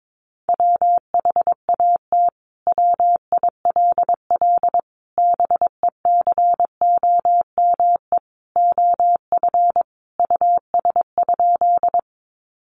Morse code, heard as a keyed tone, about 22 wpm, 700 Hz. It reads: WHAT WILL BECOME OF US?
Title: Sentences from Top 600 Words 22wpm